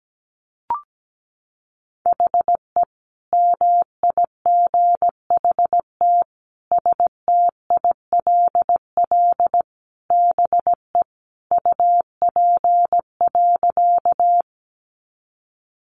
Morse code, heard as a keyed tone, about 17 wpm, 700 Hz. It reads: HE MIGHT STILL BE UP.
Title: Sentences from Top 300 Words 17wpm